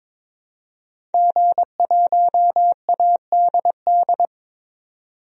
Morse code, heard as a keyed tone, about 22 wpm, 700 Hz. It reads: G1ADD
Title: DXCC-100 Call Signs - 5 Characters - 22wpm